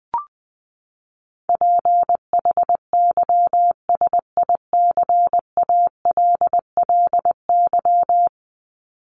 Morse code, heard as a keyed tone, about 20 wpm, 700 Hz. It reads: PHYSICALLY
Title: Top 1501-1750 Words 20wpm